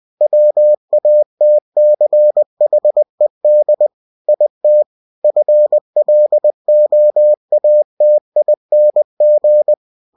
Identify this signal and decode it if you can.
Morse code, heard as a keyed tone, about 20 wpm, 600 Hz. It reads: WATCHED IT FLOATING
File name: cw-books / winnie-the-pooh / winnie-20wpm-page-0092